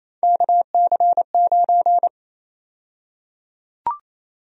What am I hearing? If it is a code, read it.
Morse code, heard as a keyed tone, about 28 wpm, 700 Hz. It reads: KC9
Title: USA Call Sign Prefixes - 28wpm